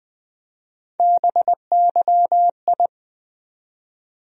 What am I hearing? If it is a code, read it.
Morse code, heard as a keyed tone, about 20 wpm, 700 Hz. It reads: BYI
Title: USA Call Sign Postfixes - 3 Letters 20wpm